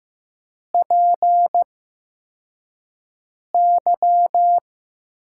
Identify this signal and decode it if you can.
Morse code, heard as a keyed tone, about 15 wpm, spaced as for 4 wpm, 700 Hz. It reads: PY
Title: USA Call Signs - 6 Characters - 8x Character Spacing - 15wpm